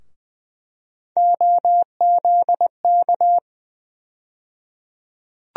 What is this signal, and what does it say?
Morse code, heard as a keyed tone, about 20 wpm, 700 Hz. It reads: OZK
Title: USA Call Sign Postfixes - 3 Letters 20wpm